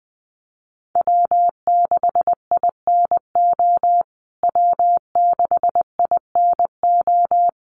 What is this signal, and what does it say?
Morse code, heard as a keyed tone, about 20 wpm, 700 Hz. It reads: W6INO W6INO
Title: USA Call Signs Repeated 2x 20wpm